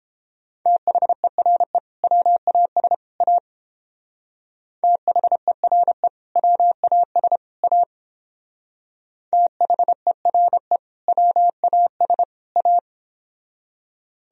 Morse code, heard as a keyed tone, about 30 wpm, 700 Hz. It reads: THERE WAS A THERE WAS A THERE WAS A